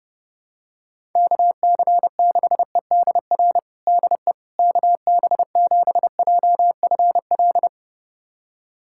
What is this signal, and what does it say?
Morse code, heard as a keyed tone, about 30 wpm, 700 Hz. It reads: KC6EDR DE KB7JFL
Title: USA Callsign DE Callsign 30wpm